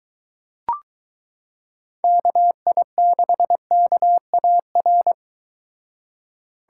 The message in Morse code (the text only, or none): KI6KAR